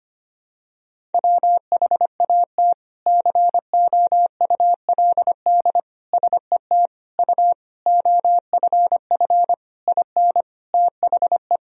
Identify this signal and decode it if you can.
Morse code, heard as a keyed tone, about 25 wpm, 700 Hz. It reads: WHAT COULD SET U OFF IN THE
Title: Sentences from Top 200 Words and Abbreviations 25wpm